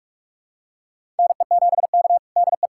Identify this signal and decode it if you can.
Morse code, heard as a keyed tone, about 46 wpm, 700 Hz. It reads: NE7K DE
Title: USA Callsign DE Callsign - 4 Characters - 45wpm